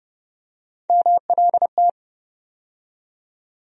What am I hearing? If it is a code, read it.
Morse code, heard as a keyed tone, about 32 wpm, 700 Hz. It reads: MLT
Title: USA Call Sign Postfixes - 3 Letters 30wpm